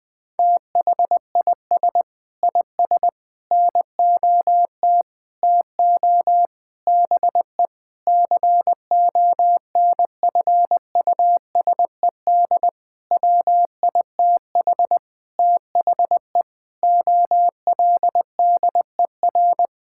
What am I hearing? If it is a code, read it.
Morse code, heard as a keyed tone, about 21 wpm, 700 Hz. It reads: THIS IS NOT TO BE CONFUSED WITH THE OLDER
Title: Sentences with 7-Word Phrases - Spoken Beginning - 20wpm